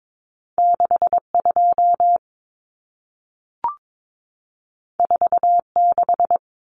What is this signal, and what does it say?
Morse code, heard as a keyed tone, about 22 wpm, 700 Hz. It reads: 62 46
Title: Numbers - 2-Digit 22wpm